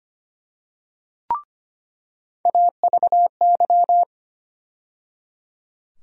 Morse code, heard as a keyed tone, about 25 wpm, 700 Hz. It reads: AVY
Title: USA Call Sign Postfixes - 3 Letters 25wpm